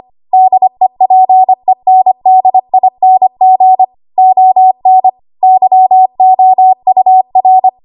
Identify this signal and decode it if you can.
Morse code, heard as a keyed tone, about 25 wpm, 750 Hz. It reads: DEPENDING ON YOUR
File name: Archive / 25 WPM / 25WPM